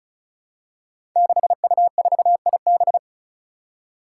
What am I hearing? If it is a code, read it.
Morse code, heard as a keyed tone, about 36 wpm, 700 Hz. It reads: 6U4IB
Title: Five Challenging Letter-Number Combinations 35wpm